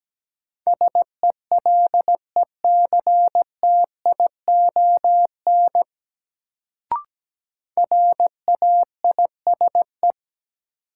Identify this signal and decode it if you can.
Morse code, heard as a keyed tone, about 17 wpm, 700 Hz. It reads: SELECTION RAISE
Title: Sentences from Top 1400 Words - New Word Drill Alt - 17wpm